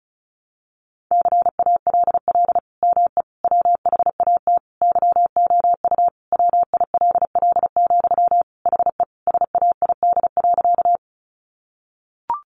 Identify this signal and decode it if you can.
Morse code, heard as a keyed tone, about 34 wpm, 700 Hz. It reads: CALL ME WHAT YOU WILL, HE SAID.
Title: Sentences from Top 200 Words 35wpm